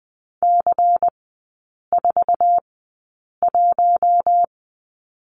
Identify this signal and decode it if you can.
Morse code, heard as a keyed tone, about 20 wpm, spaced as for 8 wpm, 700 Hz. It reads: C41
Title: 100 Lesson 17e - Letter u - Three Characters 20wpm